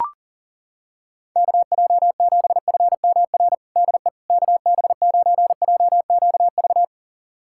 Morse code, heard as a keyed tone, about 40 wpm, 700 Hz. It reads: KJ7FMR DE KB9JQV
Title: USA Callsign DE Callsign 40wpm